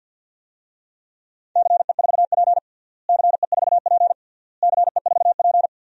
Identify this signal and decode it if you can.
Morse code, heard as a keyed tone, about 47 wpm, 700 Hz. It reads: CE4P CE4P CE4P
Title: DXCC-100 Call Signs - 4 Characters - Repeated 3x - 50wpm